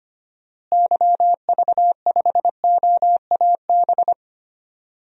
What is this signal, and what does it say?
Morse code, heard as a keyed tone, about 25 wpm, 700 Hz. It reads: YV5OAB
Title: DXCC-100 Call Signs - 6 Characters - Repeated 3x - 25wpm